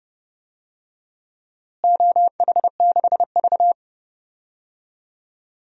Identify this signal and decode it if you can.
Morse code, heard as a keyed tone, about 32 wpm, 700 Hz. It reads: OH6V
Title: DXCC-100 Call Signs - 4 Characters - Repeated 2x - 30wpm